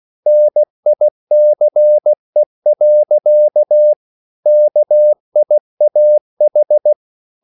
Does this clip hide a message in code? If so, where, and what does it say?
Yes: Morse code from 0.3 to 6.9 s, heard as a keyed tone, about 16 wpm, 600 Hz. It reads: NICE. KIAH